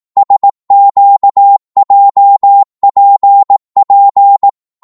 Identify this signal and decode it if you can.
Morse code, heard as a keyed tone, about 18 wpm, 800 Hz. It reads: SQJPP
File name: Morse / ltr-18wpm